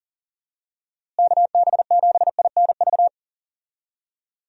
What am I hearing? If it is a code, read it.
Morse code, heard as a keyed tone, about 37 wpm, 700 Hz. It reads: KB7INV
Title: USA Call Signs - 6 Characters - Repeated 2x - 40wpm